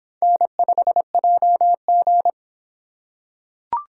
Morse code, heard as a keyed tone, about 25 wpm, 700 Hz. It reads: N5JG